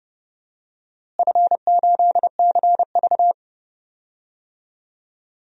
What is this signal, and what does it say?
Morse code, heard as a keyed tone, about 30 wpm, 700 Hz. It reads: F8CV
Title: DXCC-100 Call Signs - 4 Characters - Repeated 3x - 30wpm